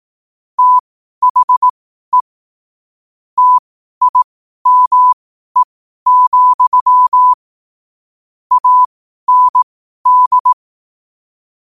Morse code, heard as a keyed tone, about 18 wpm, spaced as for 12 wpm, 1000 Hz. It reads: THE TIME, AND
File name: qotdfiles / 12wpm